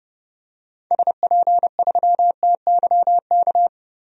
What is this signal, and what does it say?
Morse code, heard as a keyed tone, about 30 wpm, 700 Hz. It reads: SP3TYK
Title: DXCC-100 Call Signs - 6 Characters - 30wpm